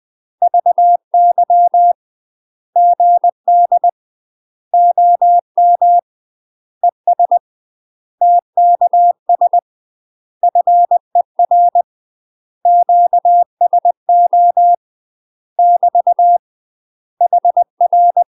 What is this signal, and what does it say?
Morse code, heard as a keyed tone, about 20 wpm, 700 Hz. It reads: VY GD OM ES TKS FER QSO <BT> HR
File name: UK MASTER QSOs TRAINING / 20WPM / UK MASTER QSOs 0072